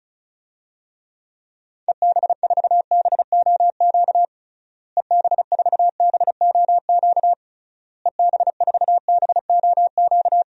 Morse code, heard as a keyed tone, about 34 wpm, 700 Hz. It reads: EB4BOQ EB4BOQ EB4BOQ
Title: DXCC-100 Call Signs - 6 Characters - Repeated 3x - 35wpm